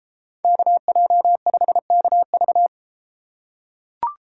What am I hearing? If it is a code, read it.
Morse code, heard as a keyed tone, about 33 wpm, 700 Hz. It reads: KJ5KV